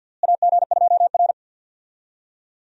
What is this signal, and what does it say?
Morse code, heard as a keyed tone, about 48 wpm, 700 Hz. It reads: AG2R
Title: USA Call Signs - 4 Characters - Repeated 3x - 50wpm